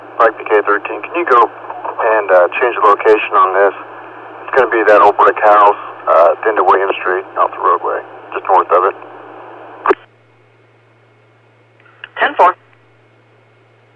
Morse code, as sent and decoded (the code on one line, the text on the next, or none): none